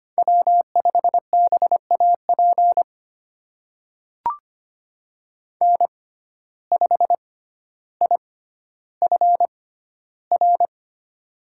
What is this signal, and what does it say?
Morse code, heard as a keyed tone, about 25 wpm, spaced as for 13 wpm, 700 Hz. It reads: W5BAP N5IFR
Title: USA Call Signs - 5 Characters - 6x Character Spacing - 25wpm